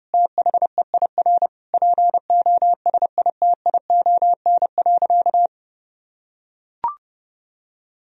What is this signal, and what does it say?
Morse code, heard as a keyed tone, about 30 wpm, 700 Hz. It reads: THEIR POSITION.
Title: Sentences from Top 1000 Words - Review Part 3 - 30wpm